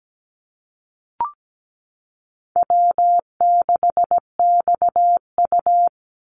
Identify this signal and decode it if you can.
Morse code, heard as a keyed tone, about 17 wpm, 700 Hz. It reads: W6XU
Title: Easy Call Signs - 17wpm